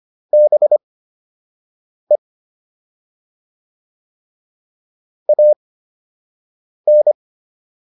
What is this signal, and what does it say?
Morse code, heard as a keyed tone, about 25 wpm, spaced as for 6 wpm, 600 Hz. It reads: BE AN